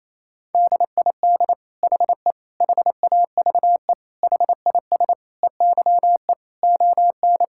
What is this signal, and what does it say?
Morse code, heard as a keyed tone, about 28 wpm, 700 Hz. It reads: DID HE HAVE HIS EYE ON
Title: Sentences from Top 1000 Words - Review Part 5 - 28wpm